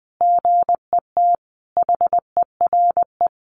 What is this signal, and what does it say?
Morse code, heard as a keyed tone, about 20 wpm, 700 Hz. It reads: GET HERE
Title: Sentences from Top 700 Words 20wpm